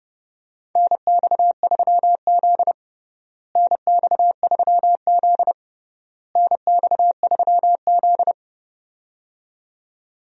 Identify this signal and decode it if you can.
Morse code, heard as a keyed tone, about 30 wpm, 700 Hz. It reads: NX3Z NX3Z NX3Z
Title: USA Call Signs - 4 Characters - Repeated 3x - 30wpm